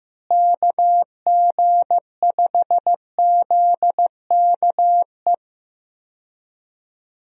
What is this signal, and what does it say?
Morse code, heard as a keyed tone, about 15 wpm, 700 Hz. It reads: KG5ZKE
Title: USA Call Signs - 6 Characters - Repeated 2x - 15wpm